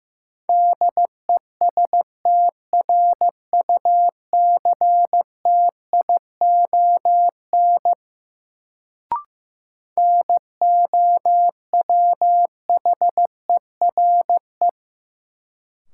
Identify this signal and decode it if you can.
Morse code, heard as a keyed tone, about 15 wpm, 700 Hz. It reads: DESTRUCTION NOWHERE
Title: Sets of 5 Words Encore - New Word Drill - 15wpm